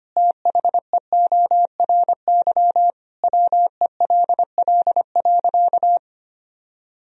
Morse code, heard as a keyed tone, about 25 wpm, 700 Hz. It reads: THEORY WELL.
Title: Sentences from Top 1000 Words - Review Part 3 - 25wpm